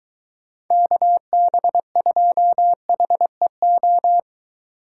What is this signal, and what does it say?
Morse code, heard as a keyed tone, about 22 wpm, 700 Hz. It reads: KB2HEO